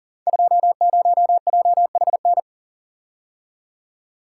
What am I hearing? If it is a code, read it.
Morse code, heard as a keyed tone, about 41 wpm, 700 Hz. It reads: 20JHN
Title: Code Groups - 5 Letter:Numbers - 40wpm